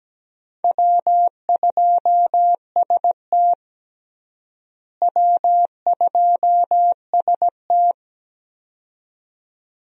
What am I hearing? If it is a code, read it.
Morse code, heard as a keyed tone, about 17 wpm, 700 Hz. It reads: W2ST W2ST
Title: USA Call Signs - 4 Characters - Repeated 2x - 17wpm